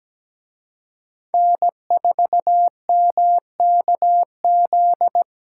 Morse code, heard as a keyed tone, about 20 wpm, 700 Hz. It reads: N4MKZ
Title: Speed Racing - USA 1x3 Call Signs 17wpm